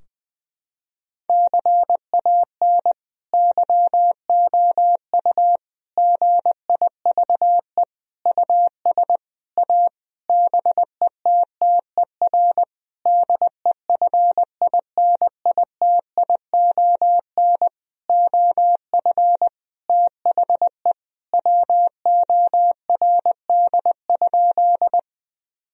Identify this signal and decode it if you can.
Morse code, heard as a keyed tone, about 20 wpm, 700 Hz. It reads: CAN YOU GIVE US A BETTER DEFINITION OF THE WORD?
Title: Sentences from Top 1000 Words - Review Part 4 - 20wpm